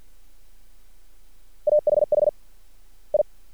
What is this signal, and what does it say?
Morse code, heard as a keyed tone, about 50 wpm, 600 Hz. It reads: AHH I
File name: archived / CRSnet